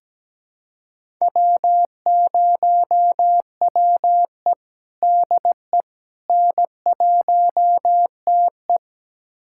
Morse code, heard as a keyed tone, about 17 wpm, 700 Hz. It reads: W0WE DE N1TE
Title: USA Callsign DE Callsign - 4 Characters - 17wpm